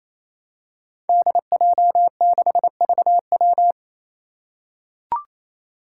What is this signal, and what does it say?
Morse code, heard as a keyed tone, about 28 wpm, 700 Hz. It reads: DJ6VW
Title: Five Challenging Letter-Number Combinations 28wpm